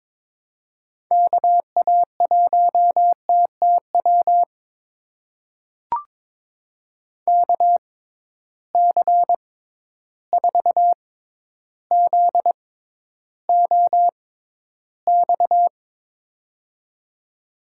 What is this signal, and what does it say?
Morse code, heard as a keyed tone, about 22 wpm, spaced as for 11 wpm, 700 Hz. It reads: KA1TTW KC4ZOX
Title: USA Call Signs - 6 Characters - 6x Character Spacing - 22wpm